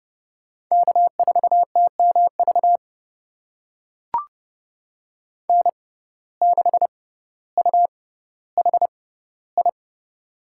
Morse code, heard as a keyed tone, about 30 wpm, spaced as for 15 wpm, 700 Hz. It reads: K4TMV N6UHI
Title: USA Call Signs - 5 Characters - 6x Character Spacing - 30wpm